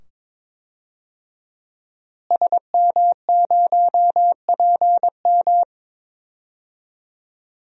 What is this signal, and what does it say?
Morse code, heard as a keyed tone, about 22 wpm, 700 Hz. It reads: SM0PM
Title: DXCC-100 Call Signs - 5 Characters - Repeated 2x - 22wpm